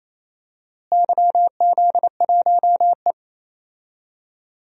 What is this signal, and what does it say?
Morse code, heard as a keyed tone, about 28 wpm, 700 Hz. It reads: YZ1E
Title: DXCC-100 Call Signs - 4 Characters - Repeated 2x - 28wpm